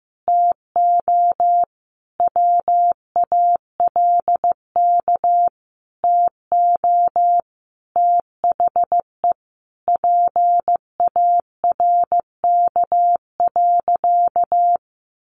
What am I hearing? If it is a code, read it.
Morse code, heard as a keyed tone, about 15 wpm, 700 Hz. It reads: TO WALK TO THE PARK.
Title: Sentences from Top 1400 Words 15wpm